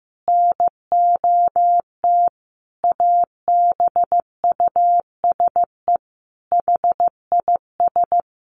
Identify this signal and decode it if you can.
Morse code, heard as a keyed tone, about 15 wpm, 700 Hz. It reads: NOT ABUSE HIS
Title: Sentences from Top 1300 Words 15wpm